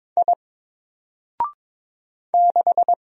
Morse code, heard as a keyed tone, about 22 wpm, 700 Hz. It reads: I 6
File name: Single-Letter-Number-22wpm